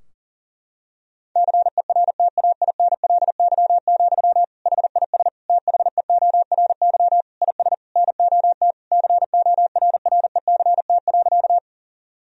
Morse code, heard as a keyed tone, about 38 wpm, 700 Hz. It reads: CERTAINLY, HIS THEORY IS NOT CORRECT.
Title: Sentences from Top 1000 Words - Review Part 3 - 40wpm